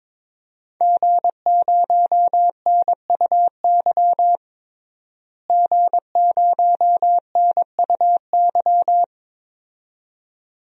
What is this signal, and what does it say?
Morse code, heard as a keyed tone, about 22 wpm, 700 Hz. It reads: G0NUY G0NUY
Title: DXCC-100 Call Signs - 5 Characters - Repeated 2x - 22wpm